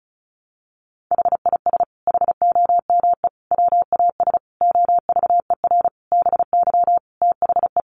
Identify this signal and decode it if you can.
Morse code, heard as a keyed tone, about 34 wpm, 700 Hz. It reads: HIS HOME WAS OVER BY THE
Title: Sentences from Top 1500 Words 35wpm